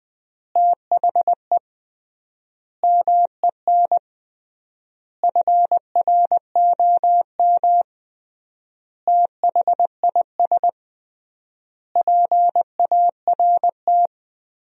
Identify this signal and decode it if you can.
Morse code, heard as a keyed tone, about 21 wpm, 700 Hz. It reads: THE MEN FROM THIS PART
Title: 3x Word-Spacing - Sentences from Top 300 Words - 20wpm